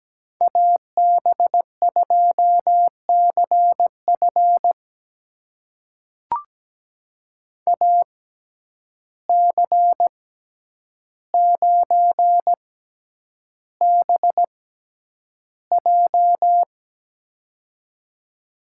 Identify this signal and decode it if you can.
Morse code, heard as a keyed tone, about 17 wpm, spaced as for 9 wpm, 700 Hz. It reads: AB2CF AC9BJ
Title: USA Call Signs - 5 Characters - 6x Character Spacing - 17wpm